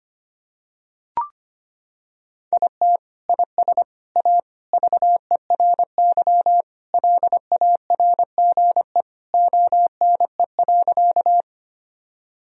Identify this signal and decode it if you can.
Morse code, heard as a keyed tone, about 25 wpm, 700 Hz. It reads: IT IS A VERY LARGE ONE.
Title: Sentences from Top 1000 Words - Review Part 1 - 25wpm